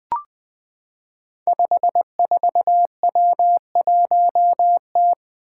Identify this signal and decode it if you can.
Morse code, heard as a keyed tone, about 20 wpm, 700 Hz. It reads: 54W1T
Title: Five Challenging Letter-Number Combinations 20wpm